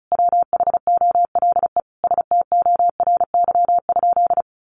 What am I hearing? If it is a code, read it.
Morse code, heard as a keyed tone, about 34 wpm, 700 Hz. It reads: WHOLE STORY?
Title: Sentences from Top 400 Words 35wpm